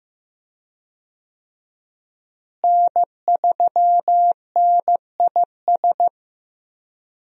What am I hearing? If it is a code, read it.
Morse code, heard as a keyed tone, about 15 wpm, 700 Hz. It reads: N3NIS
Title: USA Call Signs - 5 Characters - Repeated 3x - 15wpm